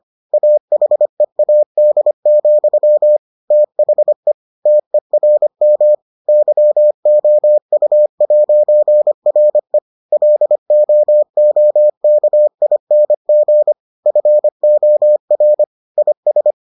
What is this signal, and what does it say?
Morse code, heard as a keyed tone, about 25 wpm, 600 Hz. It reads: AHEAD, THE TERM YOU'RE LOOKING FOR IS